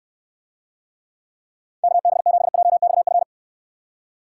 Morse code, heard as a keyed tone, about 69 wpm, 700 Hz. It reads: KB7JXF